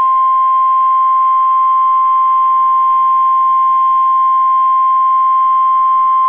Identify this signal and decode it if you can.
Emergency Alert System (SAME) tones (the 1050 Hz attention tone). It is heard from 1.1 to 5.3 s.